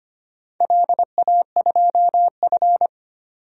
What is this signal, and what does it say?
Morse code, heard as a keyed tone, about 25 wpm, 700 Hz. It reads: LA2F
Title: DXCC-100 Call Signs - 4 Characters - 25wpm